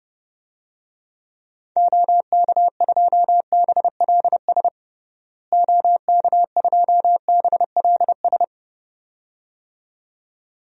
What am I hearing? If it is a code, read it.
Morse code, heard as a keyed tone, about 30 wpm, 700 Hz. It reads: OK2BLS OK2BLS
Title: DXCC-100 Call Signs - 6 Characters - Repeated 2x - 30wpm